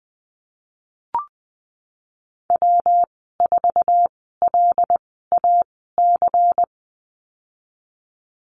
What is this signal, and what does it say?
Morse code, heard as a keyed tone, about 20 wpm, spaced as for 15 wpm, 700 Hz. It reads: W4LAC
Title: 161 Lesson 26g - Letter g - Call Signs 20wpm